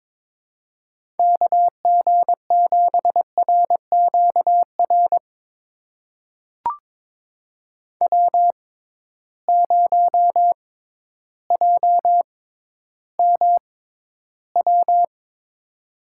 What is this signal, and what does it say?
Morse code, heard as a keyed tone, about 22 wpm, spaced as for 12 wpm, 700 Hz. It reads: KG7RQR W0JMW
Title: USA Call Signs - 6x Character Spacing - 22wpm